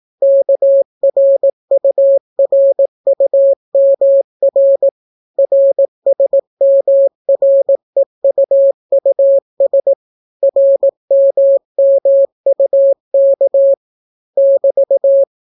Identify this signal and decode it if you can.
Morse code, heard as a keyed tone, about 18 wpm, 550 Hz. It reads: KRURUMR RSMREUUS RMMUK <BT>